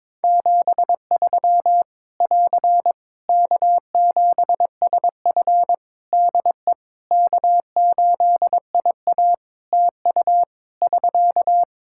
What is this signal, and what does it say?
Morse code, heard as a keyed tone, about 22 wpm, 700 Hz. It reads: 73 <AR> K7SF DE K8IA TU <SK>
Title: Practice QSO - Exchange 4 - Ending - 22wpm